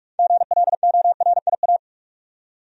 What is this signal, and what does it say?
Morse code, heard as a keyed tone, about 44 wpm, 700 Hz. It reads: GLORIA
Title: USA Common Names - 45wpm